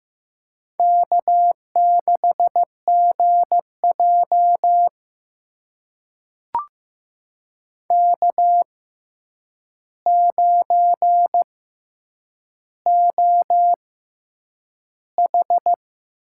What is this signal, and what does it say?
Morse code, heard as a keyed tone, about 15 wpm, spaced as for 8 wpm, 700 Hz. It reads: K6GJ K9OH
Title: USA Call Signs - 4 Characters - 6x Character Spacing - 15wpm